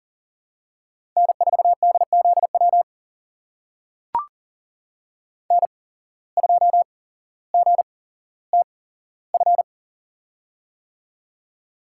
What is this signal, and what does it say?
Morse code, heard as a keyed tone, about 40 wpm, spaced as for 17 wpm, 700 Hz. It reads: N4DZW N2GTF
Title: USA Call Signs - 5 Characters - 8x Character Spacing - 40wpm